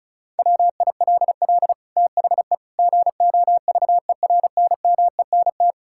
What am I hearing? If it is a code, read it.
Morse code, heard as a keyed tone, about 34 wpm, 700 Hz. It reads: WILL THE GOVERNMENT
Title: Sentences from Top 1000 Words - Review Part 5 - 35wpm